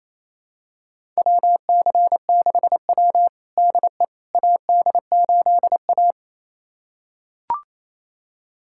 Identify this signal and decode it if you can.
Morse code, heard as a keyed tone, about 28 wpm, 700 Hz. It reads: WC6W DE AD8A
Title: USA Callsign DE Callsign - 4 Characters - 28wpm